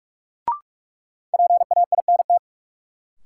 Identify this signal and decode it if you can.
Morse code, heard as a keyed tone, about 46 wpm, 700 Hz. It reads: PAINT
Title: Sentences from Top 1000 Words - Review Part 6 - New Word Drill - 45wpm